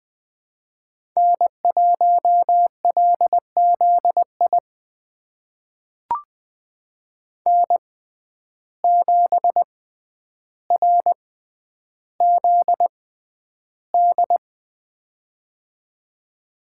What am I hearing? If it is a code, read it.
Morse code, heard as a keyed tone, about 20 wpm, spaced as for 10 wpm, 700 Hz. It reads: N1LZI N7RZD
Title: USA Call Signs - 5 Characters - 6x Character Spacing - 20wpm